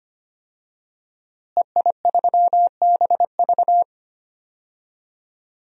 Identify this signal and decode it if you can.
Morse code, heard as a keyed tone, about 25 wpm, 700 Hz. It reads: EI3BV
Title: DXCC-100 Call Signs - 5 Characters - Repeated 3x - 25wpm